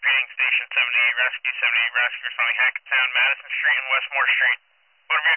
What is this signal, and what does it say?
Emergency Alert System (SAME) tones, heard as harsh buzzing bursts, with speech. It is heard from 0.9 to 4.6 s.